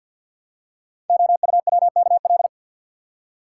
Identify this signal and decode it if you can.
Morse code, heard as a keyed tone, about 48 wpm, 700 Hz. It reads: OUWKL
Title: Code Groups - 5 Letters - 50wpm